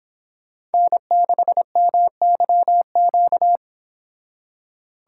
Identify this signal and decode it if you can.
Morse code, heard as a keyed tone, about 32 wpm, 700 Hz. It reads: N6MYQ